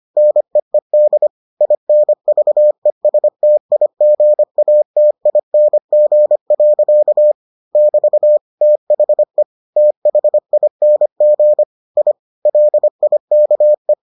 Morse code, heard as a keyed tone, about 25 wpm, 600 Hz. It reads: NEED INVESTIGATING. <BT> THE THING I LIKE